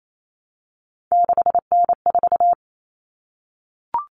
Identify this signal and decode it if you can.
Morse code, heard as a keyed tone, about 28 wpm, 700 Hz. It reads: 6N4
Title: Common Cut Numbers - 3-Digit 28wpm